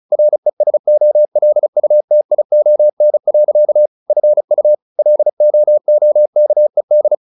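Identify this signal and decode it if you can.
Morse code, heard as a keyed tone, about 34 wpm, 600 Hz. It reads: RESOLUTION. FU LOOKED
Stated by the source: Morse practice at 35 wpm from CW med Gnister, sending Ken Follett's novel Never in English.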